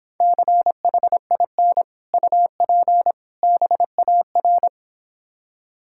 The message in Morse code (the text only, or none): CHIN UP BAR